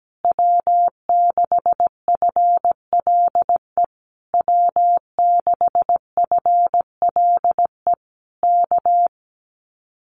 Morse code, heard as a keyed tone, about 17 wpm, 700 Hz. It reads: W6FLE W6FLE K
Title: Common QSO Phrases Encore 17wpm